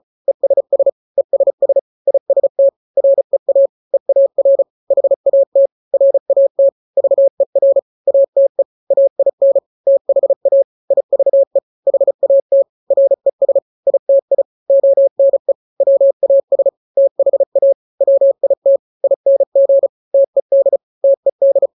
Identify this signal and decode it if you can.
Morse code, heard as a keyed tone, about 34 wpm, 550 Hz. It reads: ESS ESS IST REA EAR HAT RAT VER ATE AIN THA IVE HAT RES ITI ONE WAS THA WIT ING TED TED
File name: 3-grams 35 30 WPM